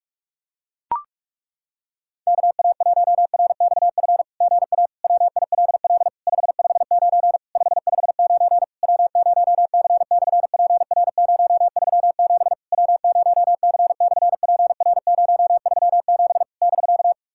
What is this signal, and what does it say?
Morse code, heard as a keyed tone, about 46 wpm, 700 Hz. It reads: KA1RXF GA WILL 559 559 W0C/PR037 W0C/PR037 <BK>
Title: SOTA QSO Practice 45wpm